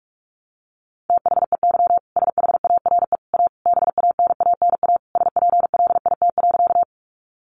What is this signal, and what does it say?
Morse code, heard as a keyed tone, about 44 wpm, 700 Hz. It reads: THEY SHARE A BANANA SPLIT.
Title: Additional Easy Phrases 45wpm